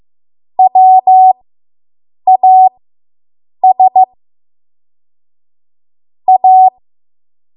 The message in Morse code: WAS A